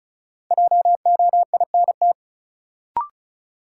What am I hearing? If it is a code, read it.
Morse code, heard as a keyed tone, about 34 wpm, 700 Hz. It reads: JOINT
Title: Sentences from Top 1300 Words - New Word Drill Alt - 35wpm